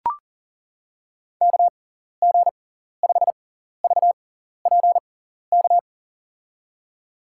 Morse code, heard as a keyed tone, about 40 wpm, spaced as for 14 wpm, 700 Hz. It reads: KG5VPK
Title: USA Call Signs - 6 Characters - 6x Character Spacing - 40wpm